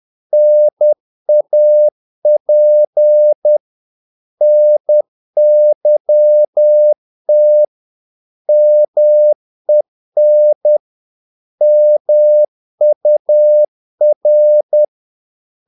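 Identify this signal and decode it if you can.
Morse code, heard as a keyed tone, about 10 wpm, 600 Hz. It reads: NAP NYT MEN MUR
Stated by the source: Morse practice of short Danish words at 10 wpm, from CW med Gnister.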